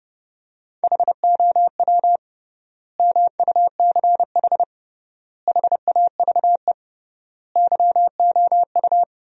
Morse code, heard as a keyed tone, about 32 wpm, 700 Hz. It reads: HOW MUCH HAVE YOU
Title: 3x Word-Spacing - Sentences from Top 300 Words - 30wpm